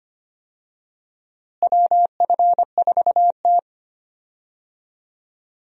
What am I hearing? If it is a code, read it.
Morse code, heard as a keyed tone, about 25 wpm, 700 Hz. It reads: WF4T
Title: USA Call Signs - 4 Characters - Repeated 3x - 25wpm